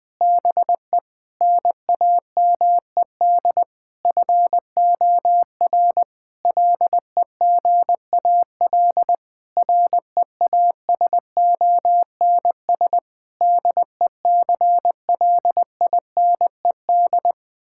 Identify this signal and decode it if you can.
Morse code, heard as a keyed tone, about 20 wpm, 700 Hz. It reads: BE NAMED FOR LEGAL REASONS DECLINED
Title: Sentences with 8-Word Phrases - 20wpm